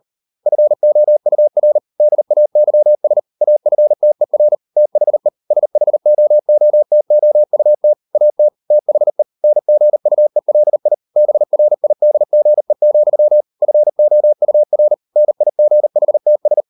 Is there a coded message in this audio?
Yes: Morse code from 0.5 to 16.6 s, heard as a keyed tone, about 38 wpm, 600 Hz. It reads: FOUR DAYS AFTER THE SHOOTOUT AT THE NGUELI BRIDGE, FOUR NIGHTS